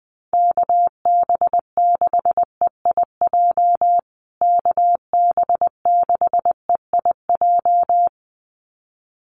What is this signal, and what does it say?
Morse code, heard as a keyed tone, about 20 wpm, 700 Hz. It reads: KB6EIJ KB6EIJ
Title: USA Call Signs Repeated 2x 20wpm